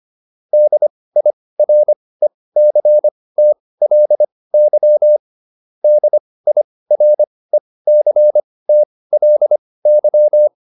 Morse code, heard as a keyed tone, about 25 wpm, spaced as for 18 wpm, 600 Hz. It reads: DIRECTLY DIRECTLY